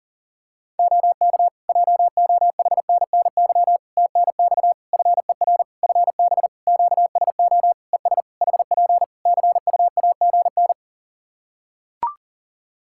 Morse code, heard as a keyed tone, about 40 wpm, 700 Hz. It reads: OK JOHNNY TNX FER FB QSO ES HP CUAGN
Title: Practice QSO - Exchange 4 - Ending - 40wpm